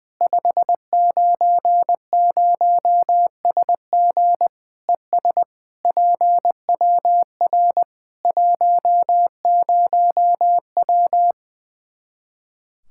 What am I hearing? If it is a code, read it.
Morse code, heard as a keyed tone, about 20 wpm, 700 Hz. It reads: 590SG ES PWR 10W
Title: Practice QSO - Exchange 2 - Info - 20wpm